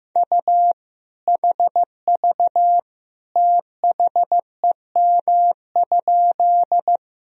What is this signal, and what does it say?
Morse code, heard as a keyed tone, about 15 wpm, 700 Hz. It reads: U HV THEM?
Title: Sentences from Top 100 Words and Abbreviations 15wpm